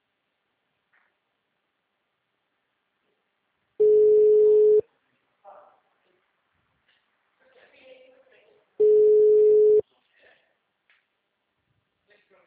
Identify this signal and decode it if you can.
Ringback tone (Europe, 425 Hz), with speech.